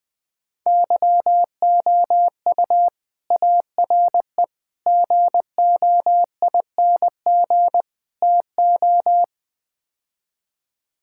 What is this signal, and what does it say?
Morse code, heard as a keyed tone, about 21 wpm, 690 Hz. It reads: YOU ARE GOING TO
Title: Build A Phrase - 4-Word Phrases - 20wpm